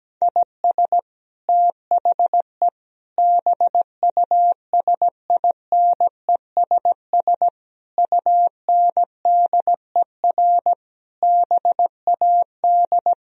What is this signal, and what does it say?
Morse code, heard as a keyed tone, about 17 wpm, 700 Hz. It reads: IS THE BUSINESS UNDER BAD
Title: Sentences from Top 1000 Words - Review Part 4 - 17wpm